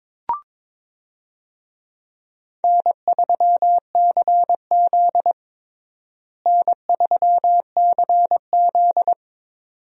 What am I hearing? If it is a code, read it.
Morse code, heard as a keyed tone, about 22 wpm, 700 Hz. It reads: N3CZ N3CZ
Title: USA Call Signs - 4 Characters - Repeated 2x - 22wpm